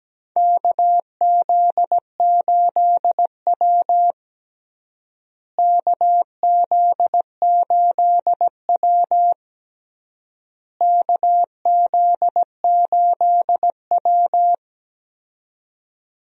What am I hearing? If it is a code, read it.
Morse code, heard as a keyed tone, about 17 wpm, 700 Hz. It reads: KZ8W KZ8W KZ8W
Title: USA Call Signs - 4 Characters - Repeated 3x - 17wpm